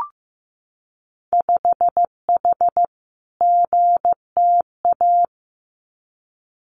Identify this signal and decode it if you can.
Morse code, heard as a keyed tone, about 15 wpm, 700 Hz. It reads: 5H GTA
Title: Winter Field Day Exchanges Encore 15wpm